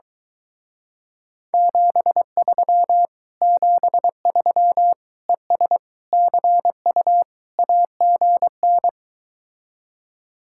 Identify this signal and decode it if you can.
Morse code, heard as a keyed tone, about 23 wpm, 700 Hz. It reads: 73 73 ES CU AGN